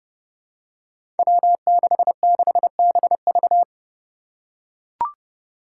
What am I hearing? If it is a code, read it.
Morse code, heard as a keyed tone, about 30 wpm, 700 Hz. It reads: W66BV
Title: Five Challenging Letter-Number Combinations 30wpm